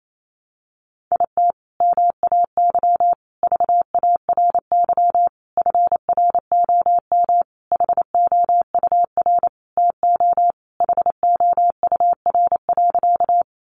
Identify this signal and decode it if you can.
Morse code, heard as a keyed tone, about 28 wpm, 700 Hz. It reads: IT MAY VARY FROM HOUR TO HOUR.
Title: Sentences from Top 1200 Words 28wpm